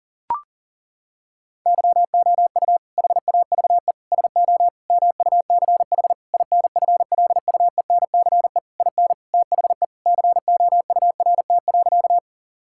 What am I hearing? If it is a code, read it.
Morse code, heard as a keyed tone, about 40 wpm, 700 Hz. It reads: YOU HAVE SO MUCH INFLUENCE IN THE COURT.
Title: Sentences from Top 1600 Words 40wpm